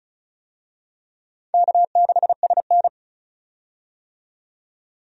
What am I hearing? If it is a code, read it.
Morse code, heard as a keyed tone, about 36 wpm, 700 Hz. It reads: K6SN
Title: USA Call Signs - 4 Characters - 35wpm